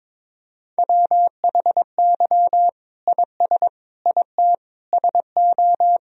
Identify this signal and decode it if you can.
Morse code, heard as a keyed tone, about 22 wpm, 700 Hz. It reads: WHY IS IT SO
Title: Sentences from Top 1000 Words - Review Part 1 - 22wpm